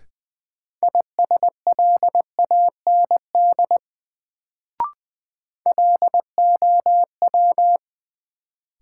Morse code, heard as a keyed tone, about 21 wpm, 700 Hz. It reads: ISLAND LOW
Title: Sentences from Top 500 Words - New Word Drill - 20wpm